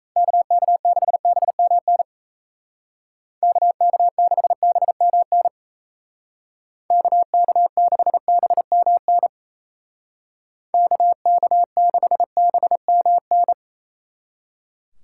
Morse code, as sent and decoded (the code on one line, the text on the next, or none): -.- -.- -.... -... -- -. / -.- -.- -.... -... -- -. / -.- -.- -.... -... -- -. / -.- -.- -.... -... -- -.
KK6BMN KK6BMN KK6BMN KK6BMN